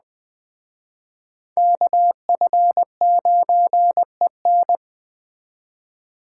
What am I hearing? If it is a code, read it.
Morse code, heard as a keyed tone, about 20 wpm, 700 Hz. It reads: KF9EN
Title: Speed Racing - USA Call Signs 17wpm